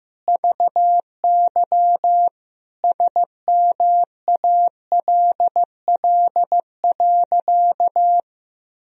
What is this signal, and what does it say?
Morse code, heard as a keyed tone, about 15 wpm, 700 Hz. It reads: VY SMALL.
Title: Sentences from Top 400 Words and Abbreviations 15wpm